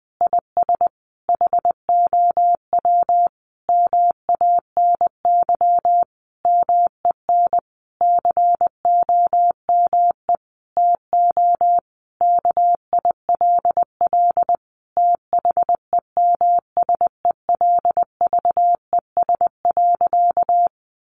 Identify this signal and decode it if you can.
Morse code, heard as a keyed tone, about 20 wpm, 700 Hz. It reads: IS HOW MANY MEN COME TO KILL THEMSELVES.
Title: Sentences from Top 1400 Words 20wpm